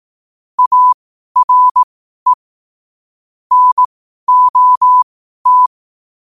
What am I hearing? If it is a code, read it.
Morse code, heard as a keyed tone, about 18 wpm, spaced as for 12 wpm, 1000 Hz. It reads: ARE NOT
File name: qotdfiles / 12wpm